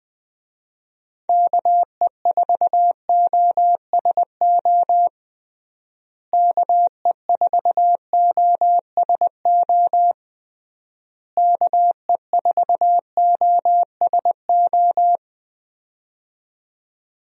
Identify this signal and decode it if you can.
Morse code, heard as a keyed tone, about 20 wpm, 700 Hz. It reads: KE4OSO KE4OSO KE4OSO
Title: USA Call Signs - 6 Characters - Repeated 3x - 20wpm